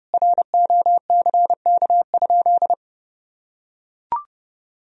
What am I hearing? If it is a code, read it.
Morse code, heard as a keyed tone, about 30 wpm, 700 Hz. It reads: ROCK?
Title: Sentences from Top 1000 Words - Review Part 2 - 30wpm